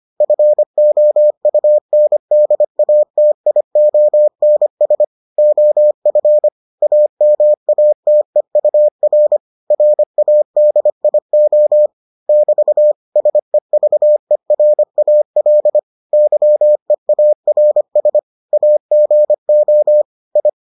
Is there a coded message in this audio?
Yes: Morse code from 0.2 to 20.5 s, heard as a keyed tone, about 25 wpm, 600 Hz. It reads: FOUNDATIONS OF AMATEUR RADIO <BT> SEVERAL YEARS AGO I